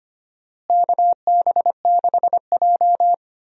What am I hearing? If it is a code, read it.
Morse code, heard as a keyed tone, about 25 wpm, 700 Hz. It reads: KB6J
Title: USA Call Signs - 4 Characters - Repeated 2x - 25wpm